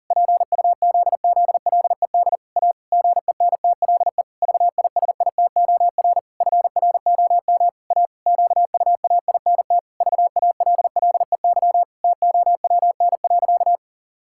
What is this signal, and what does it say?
Morse code, heard as a keyed tone, about 40 wpm, 700 Hz. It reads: PUZZLED A GENTLE VISITOR FROM A QUAINT VALLEY TOWN.
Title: Pangram Challenge v2 - 40wpm